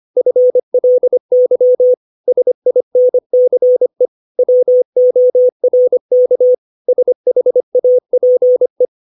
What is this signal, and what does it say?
Morse code, heard as a keyed tone, about 25 wpm, 500 Hz. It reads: FLY SINCE WORK SHAPE
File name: 632 words / 632 Words 25WPM